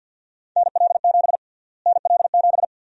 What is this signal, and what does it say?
Morse code, heard as a keyed tone, about 51 wpm, 700 Hz. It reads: NL7 NL7
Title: USA Call Sign Prefixes - Repeated 2x - 50wpm